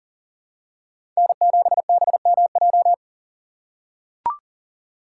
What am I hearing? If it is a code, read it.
Morse code, heard as a keyed tone, about 40 wpm, 700 Hz. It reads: N7BMJ
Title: USA Call Signs - 5 Characters - 2x Character Spacing - 40wpm